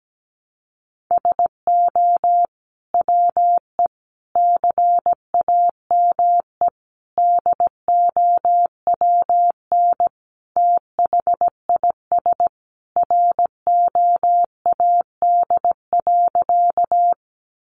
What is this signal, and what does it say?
Morse code, heard as a keyed tone, about 17 wpm, 700 Hz. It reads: SO WE CAME DOWN THIS ROAD.
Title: Sentences from Top 400 Words 17wpm